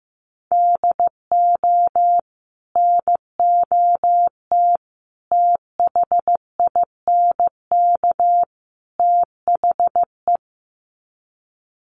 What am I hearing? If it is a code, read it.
Morse code, heard as a keyed tone, about 15 wpm, 700 Hz. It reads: DO NOT THINK THE
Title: Sets of 4 Words 15wpm